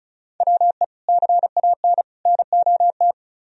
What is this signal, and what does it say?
Morse code, heard as a keyed tone, about 47 wpm, spaced as for 41 wpm, 700 Hz. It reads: WE CAN NOT
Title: Speed-Racing - Sets of 3 Words 35wpm